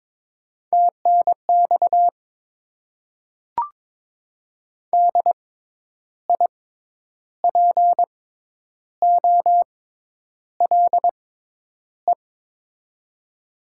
Morse code, heard as a keyed tone, about 22 wpm, spaced as for 9 wpm, 700 Hz. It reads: TNX DIPOLE
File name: Top 90 QSO Elements - 6x Character Spacing - 22wpm